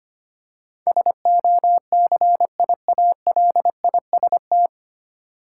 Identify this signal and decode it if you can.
Morse code, heard as a keyed tone, about 25 wpm, 700 Hz. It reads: SOCIALIST
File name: Nine Letter Words - 25wpm